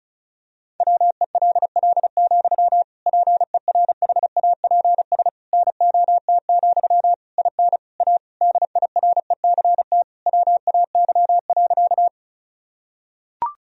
Morse code, heard as a keyed tone, about 34 wpm, 700 Hz. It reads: WELL, PERHAPS NOT, IN A DIRECT WAY.
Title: Sentences from Top 1000 Words - Review Part 2 - 35wpm